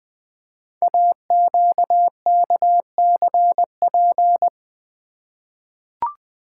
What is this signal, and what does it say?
Morse code, heard as a keyed tone, about 20 wpm, 700 Hz. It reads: AQKCP
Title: Code Groups - 5 Letters - 20wpm